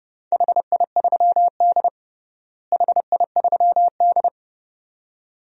Morse code, heard as a keyed tone, about 30 wpm, 700 Hz. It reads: HI3D HI3D
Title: DXCC-100 Call Signs - 4 Characters - Repeated 2x - 30wpm